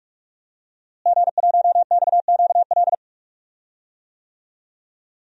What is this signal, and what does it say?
Morse code, heard as a keyed tone, about 46 wpm, 700 Hz. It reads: G1XQL
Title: DXCC-100 Call Signs - 5 Characters - Repeated 3x - 45wpm